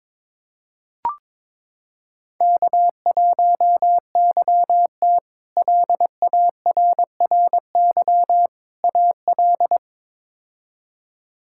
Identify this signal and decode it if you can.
Morse code, heard as a keyed tone, about 22 wpm, 700 Hz. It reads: K1YT LARRY AL
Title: K1USN SST QSO Practice - 22wpm